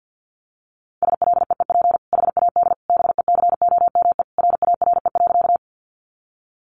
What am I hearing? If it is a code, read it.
Morse code, heard as a keyed tone, about 50 wpm, 700 Hz. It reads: SLEEP HAD BECOME RARE.
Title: Sentences from Top 1100 Words 50wpm